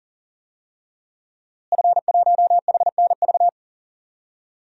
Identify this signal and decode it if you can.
Morse code, heard as a keyed tone, about 40 wpm, 700 Hz. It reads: F1HNV
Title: DXCC-100 Call Signs - 5 Characters - Repeated 2x - 40wpm